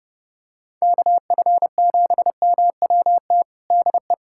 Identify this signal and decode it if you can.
Morse code, heard as a keyed tone, about 30 wpm, 700 Hz. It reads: KF7MWT DE
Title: USA Callsign DE Callsign - 6 Characters - 30wpm